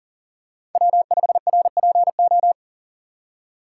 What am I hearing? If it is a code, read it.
Morse code, heard as a keyed tone, about 40 wpm, 700 Hz. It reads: W5RPO
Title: USA Call Signs - 5 Characters - 3x Character Spacing - 40wpm